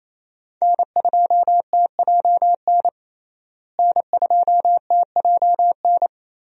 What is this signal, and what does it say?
Morse code, heard as a keyed tone, about 28 wpm, 700 Hz. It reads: N2TJN N2TJN
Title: USA Call Signs - 5 Characters - Repeated 3x - 28wpm